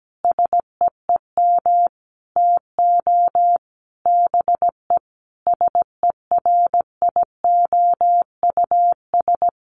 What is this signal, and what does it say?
Morse code, heard as a keyed tone, about 17 wpm, 700 Hz. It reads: SEEM TO BE SERIOUS
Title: Sentences from Top 1100 Words 17wpm